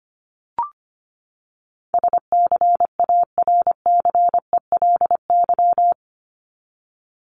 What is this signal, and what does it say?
Morse code, heard as a keyed tone, about 25 wpm, 700 Hz. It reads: SCARCELY
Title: LY Suffix - Top 100 Words with Base Words 25wpm